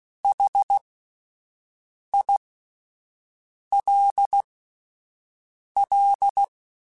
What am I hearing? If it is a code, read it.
Morse code, heard as a keyed tone, about 16 wpm, spaced as for 5 wpm, 800 Hz. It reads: HILL
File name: QSOs Spaced